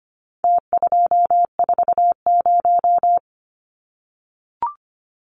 Morse code, heard as a keyed tone, about 25 wpm, 700 Hz. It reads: T240
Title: Numbers - Contest Style - 4-Digit 25wpm